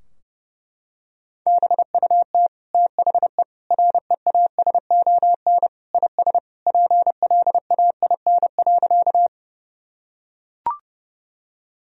Morse code, heard as a keyed tone, about 32 wpm, 700 Hz. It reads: BUT THE REASON IS PLAIN.
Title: Sentences from Top 1000 Words - Review Part 2 - 30wpm